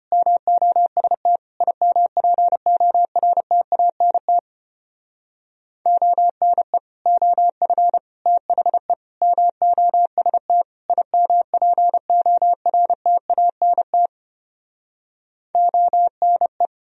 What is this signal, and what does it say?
Morse code, heard as a keyed tone, about 33 wpm, 700 Hz. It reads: MOST IMPORTANT ONE OF THE MOST IMPORTANT ONE